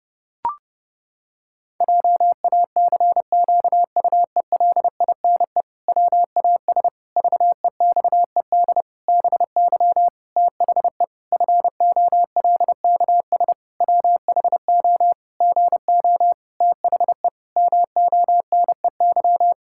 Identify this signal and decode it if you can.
Morse code, heard as a keyed tone, about 32 wpm, 700 Hz. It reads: JACQUELINE WAS VEXED BY THE FOLKS WHO GO THE MONEY
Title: Pangram Challenge v2 - 30wpm